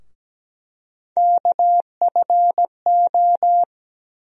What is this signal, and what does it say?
Morse code, heard as a keyed tone, about 17 wpm, 700 Hz. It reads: KFO
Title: USA Call Sign Postfixes - 3 Letters 17wpm